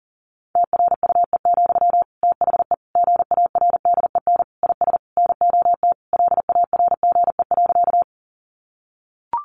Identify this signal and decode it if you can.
Morse code, heard as a keyed tone, about 40 wpm, 700 Hz. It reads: TRUE, THE GARDEN IS NOT LARGE.
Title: Sentences from Top 700 Words 40wpm